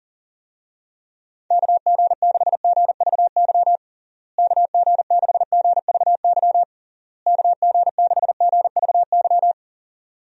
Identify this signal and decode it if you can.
Morse code, heard as a keyed tone, about 37 wpm, 700 Hz. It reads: KG6GVY KG6GVY KG6GVY
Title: USA Call Signs - 6 Characters - Repeated 3x - 40wpm